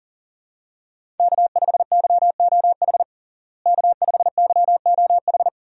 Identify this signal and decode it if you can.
Morse code, heard as a keyed tone, about 40 wpm, 700 Hz. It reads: K5YOH K5YOH
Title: USA Call Signs - 5 Characters - Repeated 2x - 40wpm